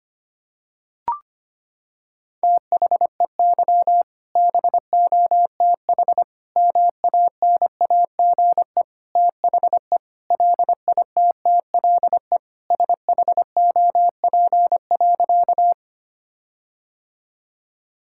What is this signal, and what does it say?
Morse code, heard as a keyed tone, about 25 wpm, 700 Hz. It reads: THEY BOTH MANAGE THE LITTLE SHOP.
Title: Sentences from Top 1600 Words 25wpm